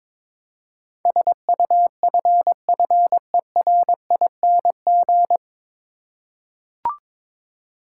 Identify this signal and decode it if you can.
Morse code, heard as a keyed tone, about 21 wpm, 700 Hz. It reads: SUFFERING
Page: Morse Code Ninja practice 22wpm